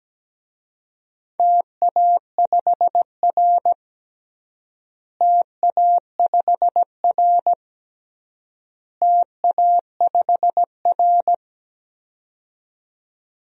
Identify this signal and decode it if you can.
Morse code, heard as a keyed tone, about 17 wpm, 700 Hz. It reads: TA5R TA5R TA5R
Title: DXCC-100 Call Signs - 4 Characters - Repeated 3x - 17wpm